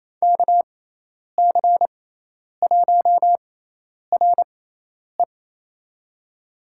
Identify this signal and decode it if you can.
Morse code, heard as a keyed tone, about 28 wpm, spaced as for 10 wpm, 700 Hz. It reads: KC1RE
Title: USA Call Signs - 5 Characters - 6x Character Spacing - 28wpm